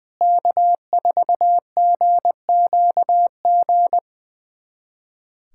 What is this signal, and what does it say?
Morse code, heard as a keyed tone, about 20 wpm, 700 Hz. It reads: K4GQG
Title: USA Callsign DE Callsign - 5 Characters - 20wpm